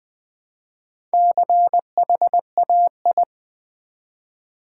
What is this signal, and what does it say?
Morse code, heard as a keyed tone, about 20 wpm, 700 Hz. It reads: CHAI
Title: Build A Word - 1 Letter At A Time - 3 to 5-Letter Words - 20wpm